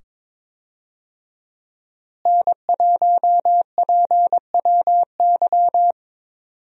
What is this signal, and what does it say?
Morse code, heard as a keyed tone, about 22 wpm, 700 Hz. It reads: N1PWY
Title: USA Call Signs - 5 Characters - Repeated 2x - 22wpm